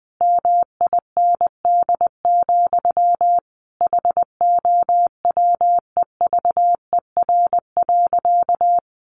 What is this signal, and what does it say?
Morse code, heard as a keyed tone, about 20 wpm, 700 Hz. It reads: MIND, HOWEVER.
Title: Sentences from Top 1100 Words 20wpm